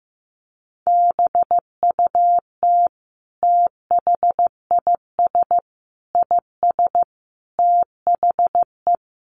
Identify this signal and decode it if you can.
Morse code, heard as a keyed tone, about 15 wpm, 700 Hz. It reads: BUT THIS IS THE
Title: Sentences from Top 200 Words 15wpm